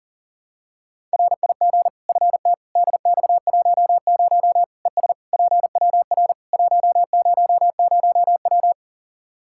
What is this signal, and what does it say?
Morse code, heard as a keyed tone, about 40 wpm, 700 Hz. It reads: RIG FT DX10 ES PWR 100W
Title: Practice QSO - Exchange 2 - Info - 40wpm